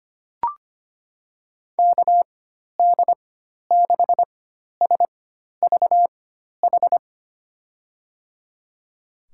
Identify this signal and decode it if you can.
Morse code, heard as a keyed tone, about 25 wpm, spaced as for 12 wpm, 700 Hz. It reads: KD6SVH
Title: USA Call Signs - 6 Characters - 4x Character Spacing - 25wpm